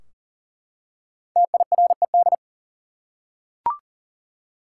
Morse code, heard as a keyed tone, about 40 wpm, 700 Hz. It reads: TIRED
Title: Five Letter Words - 40wpm